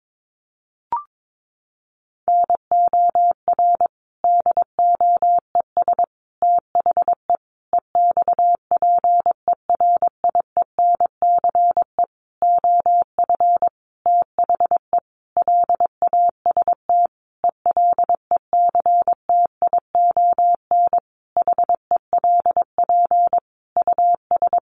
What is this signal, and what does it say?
Morse code, heard as a keyed tone, about 22 wpm, 700 Hz. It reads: NOR DOES THE EXPERIENCE OF THE LAST ELECTION HELP US
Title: Sentences from Top 1500 Words 22wpm